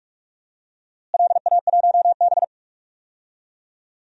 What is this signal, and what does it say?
Morse code, heard as a keyed tone, about 46 wpm, 700 Hz. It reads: LA1B
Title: DXCC-100 Call Signs - 4 Characters - Repeated 2x - 45wpm